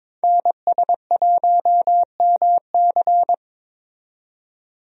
Morse code, heard as a keyed tone, about 22 wpm, 700 Hz. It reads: NS1MC
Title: USA Call Signs - 5 Characters - 22wpm